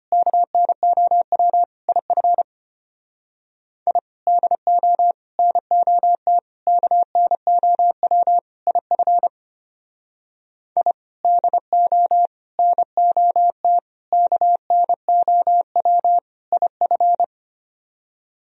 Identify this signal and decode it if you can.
Morse code, heard as a keyed tone, about 33 wpm, 700 Hz. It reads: KNOW IF I DO NOT KNOW IF I DO NOT KNOW IF